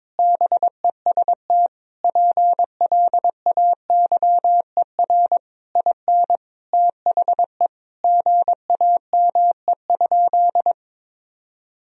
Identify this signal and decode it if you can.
Morse code, heard as a keyed tone, about 21 wpm, 700 Hz. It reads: BEST PLAYER IN THE GAME?
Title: Sentences from Top 1000 Words - Review Part 4 - 22wpm